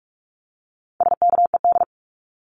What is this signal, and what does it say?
Morse code, heard as a keyed tone, about 44 wpm, 700 Hz. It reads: SKED
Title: Abbreviations 45wpm